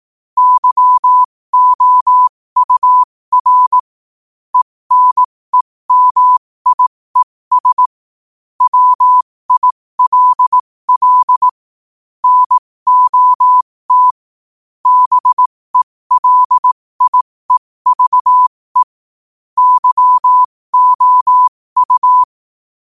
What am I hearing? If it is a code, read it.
Morse code, heard as a keyed tone, about 18 wpm, spaced as for 15 wpm, 1000 Hz. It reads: YOUR ENEMIES WILL NOT BELIEVE YOU
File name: qotdfiles / 15wpm